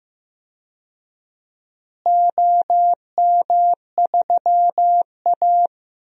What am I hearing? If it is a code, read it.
Morse code, heard as a keyed tone, about 15 wpm, 700 Hz. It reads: OM3A
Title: DXCC-100 Call Signs - 4 Characters - Repeated 3x - 15wpm